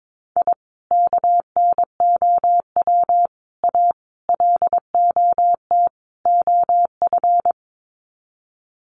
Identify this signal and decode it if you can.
Morse code, heard as a keyed tone, about 22 wpm, 700 Hz. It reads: I KNOW A LOT OF
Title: Sets of 5 Words Encore 22wpm